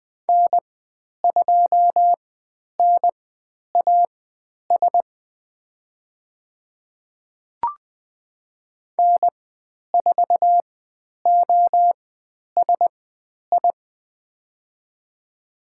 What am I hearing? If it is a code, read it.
Morse code, heard as a keyed tone, about 20 wpm, spaced as for 10 wpm, 700 Hz. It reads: N2NAS N4OSI
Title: 095 Lesson 16g - Letter c - Call Signs 20wpm